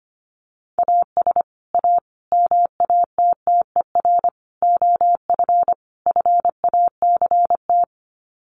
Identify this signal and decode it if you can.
Morse code, heard as a keyed tone, about 25 wpm, 700 Hz. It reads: AS A MATTER OF FACT
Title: Sets of 5 Words 25wpm